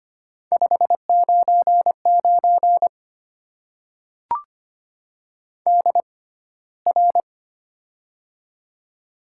Morse code, heard as a keyed tone, about 25 wpm, spaced as for 15 wpm, 700 Hz. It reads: 599 DR
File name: Top 90 QSO Elements - 6x Character Spacing - 25wpm